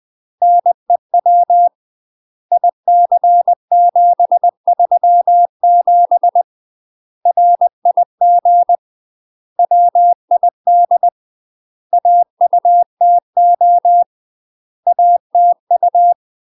Morse code, heard as a keyed tone, about 20 wpm, 700 Hz. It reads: NEW IC737 RIG WID AUTO ATU
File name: UK MASTER QSOs TRAINING / 20WPM / UK MASTER QSOs 0070